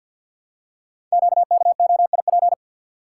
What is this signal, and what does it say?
Morse code, heard as a keyed tone, about 48 wpm, 700 Hz. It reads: QKOIP
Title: Code Groups - 5 Letters - 50wpm